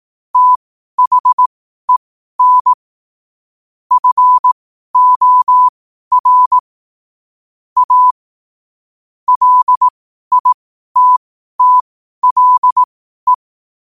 Morse code, heard as a keyed tone, about 18 wpm, spaced as for 12 wpm, 1000 Hz. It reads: THEN FOR A LITTLE
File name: qotdfiles / 12wpm